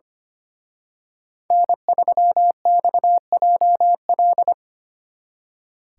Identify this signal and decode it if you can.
Morse code, heard as a keyed tone, about 29 wpm, 700 Hz. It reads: N3XJL
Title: Speed Racing - USA 1x3 Call Signs 25wpm